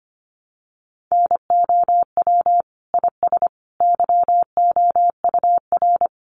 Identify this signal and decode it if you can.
Morse code, heard as a keyed tone, about 25 wpm, 700 Hz. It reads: NOW IS YOUR
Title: Sentences from Top 100 Words 25wpm